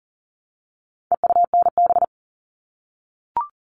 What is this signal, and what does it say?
Morse code, heard as a keyed tone, about 40 wpm, 700 Hz. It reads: EUNB
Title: All Cut Numbers - 4-Digit 40wpm